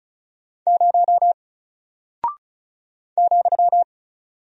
Morse code, heard as a keyed tone, about 36 wpm, 700 Hz. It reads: 0 ,
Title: Single Character (Letter:Number:Punctuation:Prosign) Alt 35wpm